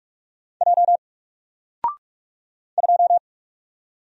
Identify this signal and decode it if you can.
Morse code, heard as a keyed tone, about 45 wpm, 700 Hz. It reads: J 2
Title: Single Letter-Number 45wpm